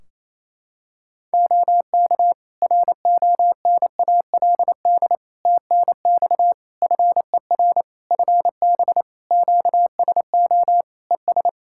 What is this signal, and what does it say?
Morse code, heard as a keyed tone, about 28 wpm, 700 Hz. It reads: OK RONALD TNX FER FB QSO ES
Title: Practice QSO - Exchange 4 - Ending - 28wpm